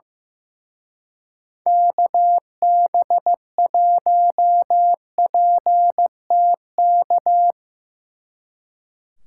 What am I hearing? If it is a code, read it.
Morse code, heard as a keyed tone, about 16 wpm, 700 Hz. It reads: KB1PTK
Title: Speed Racing - USA 2x3 Call Signs 15wpm